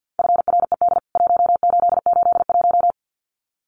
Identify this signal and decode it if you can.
Morse code, heard as a keyed tone, about 50 wpm, 700 Hz. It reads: FRED 1881
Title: CWT Exchange 50wpm